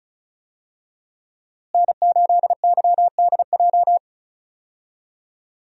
Morse code, heard as a keyed tone, about 36 wpm, 700 Hz. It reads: N8YDJ
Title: USA Call Signs - 5 Characters - Repeated 3x - 35wpm